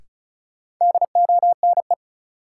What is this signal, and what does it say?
Morse code, heard as a keyed tone, about 36 wpm, 700 Hz. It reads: DONE
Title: Sentences from Top 500 Words - New Word Drill - 35wpm